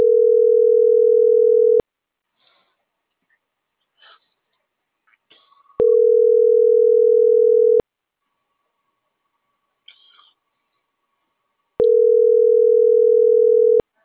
Ringback tone (North America), 440 + 480 Hz.